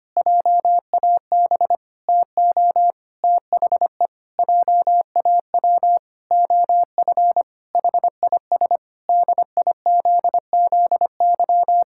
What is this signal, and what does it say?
Morse code, heard as a keyed tone, about 25 wpm, 700 Hz. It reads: JAB TO THE JAW OF HIS DIZZY
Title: Pangram Challenge v2 - 25wpm